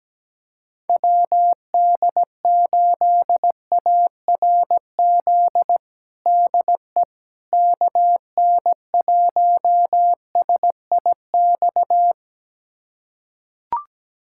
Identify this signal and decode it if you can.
Morse code, heard as a keyed tone, about 17 wpm, 700 Hz. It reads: WD8ARZ DE KN1SIX
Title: USA Callsign DE Callsign - 6 Characters - 17wpm